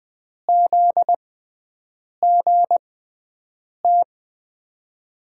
Morse code, heard as a keyed tone, about 20 wpm, spaced as for 7 wpm, 700 Hz. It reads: ZGT
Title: USA Call Signs - 5 Characters - 6x Character Spacing - 20wpm